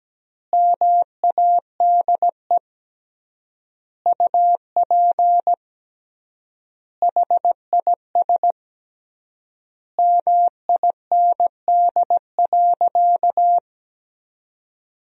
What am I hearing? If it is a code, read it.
Morse code, heard as a keyed tone, about 17 wpm, 700 Hz. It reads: MADE UP HIS MIND.
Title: 3x Word-Spacing - Sentences from Top 500 Words - 17wpm